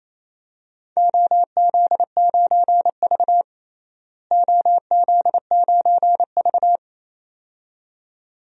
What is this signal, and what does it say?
Morse code, heard as a keyed tone, about 28 wpm, 700 Hz. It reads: OZ9V OZ9V
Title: DXCC-100 Call Signs - 4 Characters - Repeated 2x - 28wpm